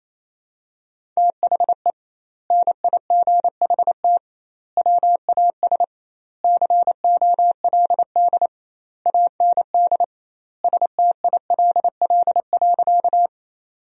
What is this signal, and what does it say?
Morse code, heard as a keyed tone, about 28 wpm, 700 Hz. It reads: THE NIGHT WAS COLD AND STILL.
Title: 2x Word-Spacing - Sentences from Top 400 Words - 28wpm